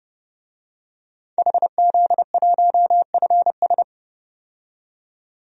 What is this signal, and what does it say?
Morse code, heard as a keyed tone, about 30 wpm, 700 Hz. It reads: HZ1FS
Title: DXCC-100 Call Signs - 5 Characters - 30wpm